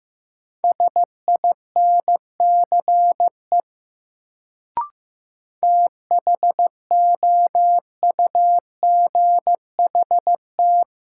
Morse code, heard as a keyed tone, about 15 wpm, 700 Hz. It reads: SINCE THOUGHT
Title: Sets of 2 Words - New Word Drill - 15wpm